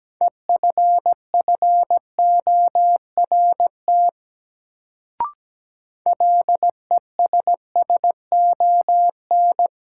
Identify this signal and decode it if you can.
Morse code, heard as a keyed tone, about 17 wpm, 700 Hz. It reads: EFFORT LESSON
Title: Sentences from Top 1200 Words - New Word Drill - 17wpm